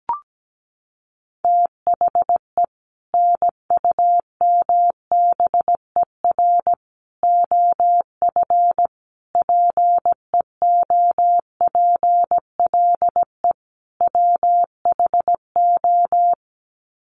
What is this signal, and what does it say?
Morse code, heard as a keyed tone, about 17 wpm, 700 Hz. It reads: THE NUMBER OF PEOPLE WHO
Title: Sets of 5 Words Encore 17wpm